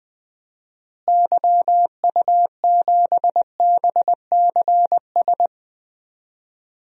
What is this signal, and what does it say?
Morse code, heard as a keyed tone, about 21 wpm, 700 Hz. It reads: YU7BCS
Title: DXCC-100 Call Signs - 6 Characters - Repeated 2x - 20wpm